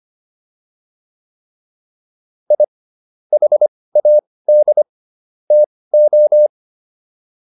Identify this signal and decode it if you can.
Morse code, heard as a keyed tone, about 25 wpm, spaced as for 18 wpm, 600 Hz. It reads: I HAD TO